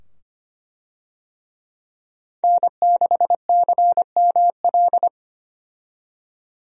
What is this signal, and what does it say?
Morse code, heard as a keyed tone, about 25 wpm, 700 Hz. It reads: N6CML
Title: USA Call Signs - 5 Characters - 25wpm